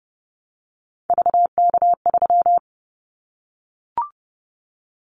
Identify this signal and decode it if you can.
Morse code, heard as a keyed tone, about 31 wpm, 700 Hz. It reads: VK3
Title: Callsign Prefixes for Top 100 DXCC - 30wpm